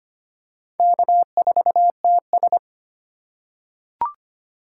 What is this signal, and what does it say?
Morse code, heard as a keyed tone, about 25 wpm, 700 Hz. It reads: K4TS
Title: USA Call Signs - 4 Characters - 2x Character Spacing - 25wpm